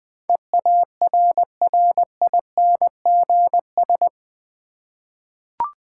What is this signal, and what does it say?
Morse code, heard as a keyed tone, about 20 wpm, 690 Hz. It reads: EARRINGS
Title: Difficult Words with Hints - 20wpm